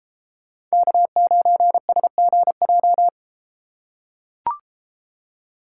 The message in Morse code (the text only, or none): K9SGJ